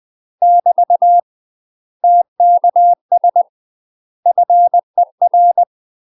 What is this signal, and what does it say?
Morse code, heard as a keyed tone, about 20 wpm, 700 Hz. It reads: <BT> TKS FER
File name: UK MASTER QSOs TRAINING / 20WPM / UK MASTER QSOs 0055